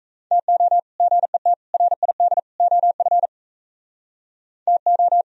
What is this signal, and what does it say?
Morse code, heard as a keyed tone, about 37 wpm, 700 Hz. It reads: TO GET RID OF TO